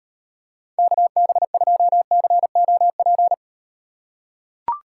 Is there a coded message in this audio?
Yes: Morse code from 0.8 to 3.3 s, heard as a keyed tone, about 38 wpm, 700 Hz. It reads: KB2COP